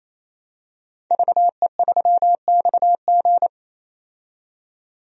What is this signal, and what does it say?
Morse code, heard as a keyed tone, about 28 wpm, 700 Hz. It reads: VE3XG
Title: DXCC-100 Call Signs - 5 Characters - 28wpm